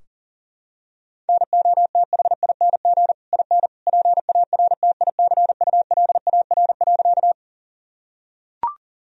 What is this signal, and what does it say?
Morse code, heard as a keyed tone, about 40 wpm, 700 Hz. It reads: NOTHING IN PARTICULAR.
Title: Sentences from Top 1000 Words - Review Part 4 - 40wpm